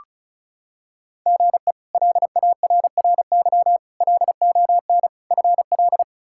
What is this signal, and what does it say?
Morse code, heard as a keyed tone, about 34 wpm, 700 Hz. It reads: GE LARRY LON FL
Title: K1USN SST QSO Practice - 35wpm